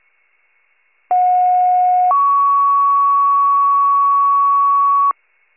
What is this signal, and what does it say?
Two-tone page, heard as a one-second tone, then a three-second tone, about 730 then 1120 Hz. It is heard from 1.1 to 5.1 s.